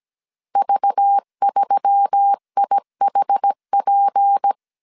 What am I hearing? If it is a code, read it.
Morse code, heard as a keyed tone, about 17 wpm, 800 Hz. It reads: V3IHP
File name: CW / DET / 1 6 1500C S B 16wpm 2619